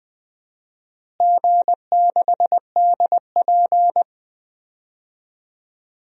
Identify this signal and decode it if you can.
Morse code, heard as a keyed tone, about 20 wpm, 700 Hz. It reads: G6DP
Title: DXCC-100 Call Signs - 4 Characters - Repeated 2x - 20wpm